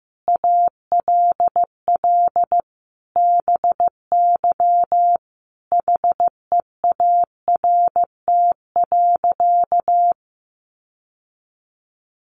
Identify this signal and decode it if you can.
Morse code, heard as a keyed tone, about 15 wpm, 700 Hz. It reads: ALL BY HEART.
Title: Sentences from Top 1200 Words 15wpm